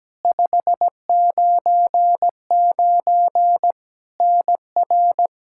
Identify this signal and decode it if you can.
Morse code, heard as a keyed tone, about 17 wpm, 700 Hz. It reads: 599 NR
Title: Practice QSO - Exchange 1 - Beginning - 17wpm